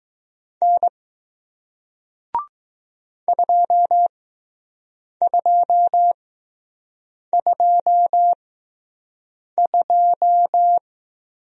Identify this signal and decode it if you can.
Morse code, heard as a keyed tone, about 20 wpm, spaced as for 6 wpm, 700 Hz. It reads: N 2222